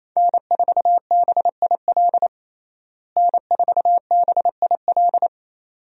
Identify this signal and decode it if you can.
Morse code, heard as a keyed tone, about 28 wpm, 700 Hz. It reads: N4BIL N4BIL
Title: USA Call Signs - 5 Characters - Repeated 2x - 28wpm